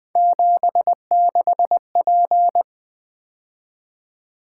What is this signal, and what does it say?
Morse code, heard as a keyed tone, about 20 wpm, 700 Hz. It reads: 76P
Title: Three Letter-Number Combinations - 20wpm